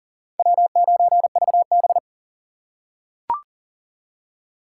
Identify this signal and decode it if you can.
Morse code, heard as a keyed tone, about 40 wpm, 700 Hz. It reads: W9VB
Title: USA Call Signs - 4x Character Spacing - 40wpm